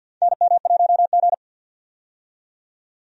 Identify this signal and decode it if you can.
Morse code, heard as a keyed tone, about 51 wpm, 700 Hz. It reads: NM1G
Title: USA Call Signs - 4 Characters - 50wpm